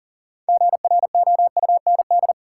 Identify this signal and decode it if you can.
Morse code, heard as a keyed tone, about 40 wpm, 700 Hz. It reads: GROUND
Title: Sets of 3 Words - New Word Drill - 40wpm